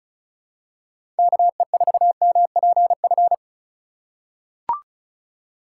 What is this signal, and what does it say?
Morse code, heard as a keyed tone, about 36 wpm, 700 Hz. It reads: KE4MPF
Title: USA Call Signs - 6 Characters - 4x Character Spacing - 35wpm